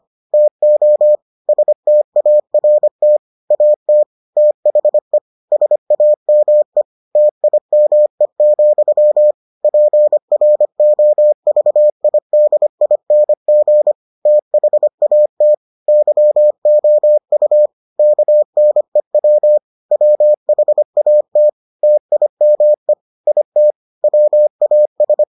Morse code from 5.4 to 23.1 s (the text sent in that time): SAME TIME, PROVIDING THAT YOU KNEW WHAT TIME